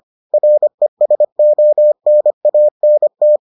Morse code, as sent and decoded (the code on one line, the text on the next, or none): .-. ... --- -. .- -. -
RESONANT